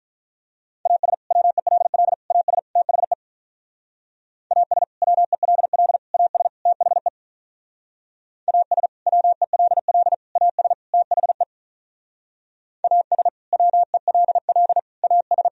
Morse code, heard as a keyed tone, about 45 wpm, 700 Hz. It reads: AS WELL AS THE AS WELL AS THE AS WELL AS THE AS WELL AS